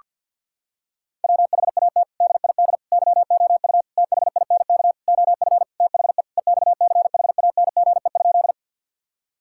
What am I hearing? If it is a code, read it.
Morse code, heard as a keyed tone, about 50 wpm, 700 Hz. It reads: WHAT DID YOU THINK OF THE EXCHANGE?
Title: Sentences from Top 1000 Words - Review Part 6 - 50wpm